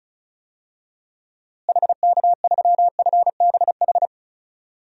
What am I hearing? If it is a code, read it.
Morse code, heard as a keyed tone, about 36 wpm, 700 Hz. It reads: HK3FBH
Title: DXCC-100 Call Signs - 6 Characters - Repeated 2x - 35wpm